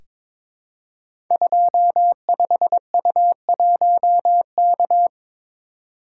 Morse code, heard as a keyed tone, about 22 wpm, 700 Hz. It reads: 25U1K
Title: Code Groups - 5 Letter:Numbers - 22wpm